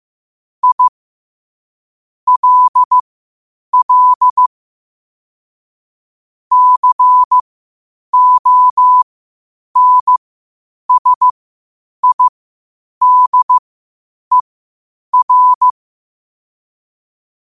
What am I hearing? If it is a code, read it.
Morse code, heard as a keyed tone, about 15 wpm, spaced as for 8 wpm, 1000 Hz. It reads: I LL CONSIDER
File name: qotdfiles / 8wpm